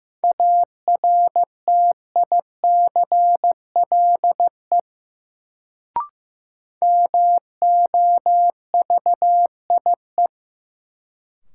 Morse code, heard as a keyed tone, about 15 wpm, 700 Hz. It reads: ARTICLE MOVIE
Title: Sentences from Top 1000 Words - Review Part 5 - New Word Drill - 15wpm